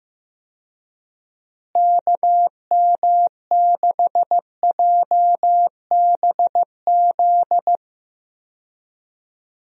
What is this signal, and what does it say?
Morse code, heard as a keyed tone, about 15 wpm, 700 Hz. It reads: KM6JBZ
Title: USA Call Signs - 6 Characters - Repeated 2x - 15wpm